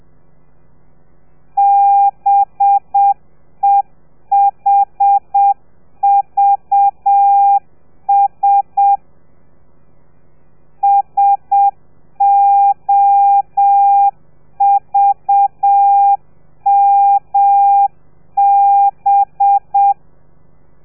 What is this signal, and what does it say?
Morse code, heard as a keyed tone, about 7 wpm, 800 Hz. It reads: BEHVS SOVMB
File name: telegrafia / i2tfj / Lez4 30